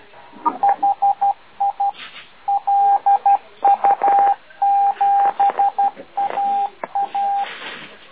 Morse code, heard as a keyed tone, about 12 wpm, 800 Hz. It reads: HI LU7AA